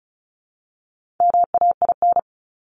Morse code, heard as a keyed tone, about 34 wpm, 700 Hz. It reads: MAIN
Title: Top 300 Words 35wpm